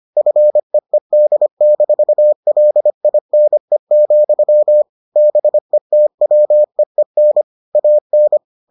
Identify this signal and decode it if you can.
Morse code, heard as a keyed tone, about 25 wpm, 600 Hz. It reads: FEED-LINE, BETWEEN AN